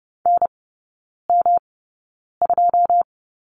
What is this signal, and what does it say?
Morse code, heard as a keyed tone, about 30 wpm, spaced as for 9 wpm, 700 Hz. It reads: NM2
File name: Lesson 18e - Letter m - Three Characters 30wpm